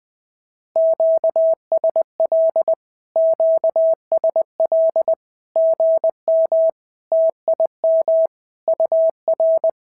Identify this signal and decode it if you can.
Morse code, heard as a keyed tone, about 20 wpm, 650 Hz. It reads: QSL QSL GM TIM UR
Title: SOTA QSO Practice 20wpm